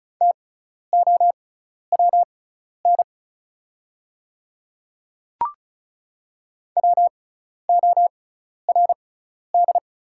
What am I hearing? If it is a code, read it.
Morse code, heard as a keyed tone, about 36 wpm, spaced as for 12 wpm, 690 Hz. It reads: TOWN WORD
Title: 6x Character Spacing - Top 500 Words - 35wpm